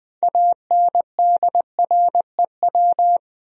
Morse code, heard as a keyed tone, about 21 wpm, 700 Hz. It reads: ANDREW
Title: USA Common Names - 20wpm